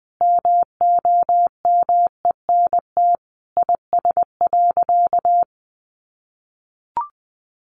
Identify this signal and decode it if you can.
Morse code, heard as a keyed tone, about 20 wpm, 700 Hz. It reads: MOMENT IS.
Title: Sentences from Top 1000 Words 20wpm